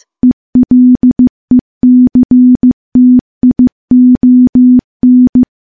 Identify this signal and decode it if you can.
Morse code, heard as a keyed tone, about 15 wpm, 250 Hz. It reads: ELECTION
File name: morse-training-15WPM